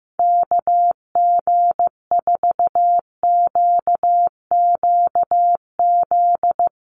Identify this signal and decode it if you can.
Morse code, heard as a keyed tone, about 15 wpm, 700 Hz. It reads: KG4QQZ
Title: USA Call Signs Repeated 2x 15wpm